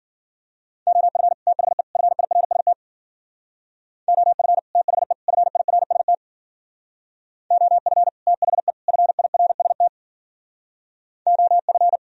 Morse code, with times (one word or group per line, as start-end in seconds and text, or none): 0.9-1.3 OF
1.5-1.8 THE
2.0-2.7 FIRST
4.1-4.6 OF
4.8-5.1 THE
5.3-6.1 FIRST
7.5-8.1 OF
8.3-8.7 THE
8.9-9.9 FIRST
11.3-12.0 OF